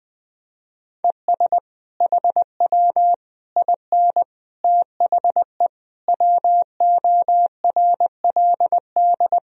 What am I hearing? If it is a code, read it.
Morse code, heard as a keyed tone, about 20 wpm, 700 Hz. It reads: ES HW IN THE WORLD
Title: Sentences from Top 200 Words and Abbreviations 20wpm